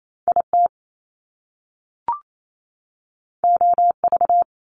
Morse code, heard as a keyed tone, about 28 wpm, 700 Hz. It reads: IT OV
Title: Top Two Letter Combinations in Words 28wpm